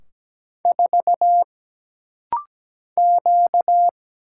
Morse code, heard as a keyed tone, about 17 wpm, 700 Hz. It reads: 4 Q
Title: Single Letter-Number Alt - Rapid-Fire 17wpm